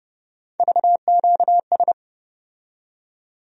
Morse code, heard as a keyed tone, about 30 wpm, 700 Hz. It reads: VQS
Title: USA Call Sign Postfixes - 3 Letters 30wpm